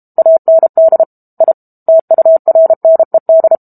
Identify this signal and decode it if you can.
Morse code, heard as a keyed tone, about 32 wpm, 650 Hz. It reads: AND I TURNED
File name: Book Two / 06-Chapter 6 - 32 WPM